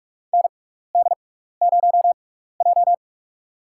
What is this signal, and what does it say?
Morse code, heard as a keyed tone, about 46 wpm, spaced as for 15 wpm, 700 Hz. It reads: ND0J
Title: USA Call Signs - 4 Characters - 6x Character Spacing - 45wpm